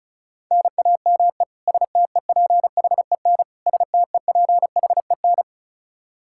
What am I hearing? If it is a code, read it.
Morse code, heard as a keyed tone, about 36 wpm, 700 Hz. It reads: NAME STEPHEN STEPHEN
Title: Practice QSO - Exchange 1 - Beginning - 35wpm